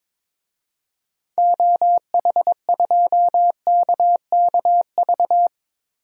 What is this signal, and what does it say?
Morse code, heard as a keyed tone, about 22 wpm, 700 Hz. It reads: OH2KKV
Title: DXCC-100 Call Signs - 6 Characters - Repeated 3x - 22wpm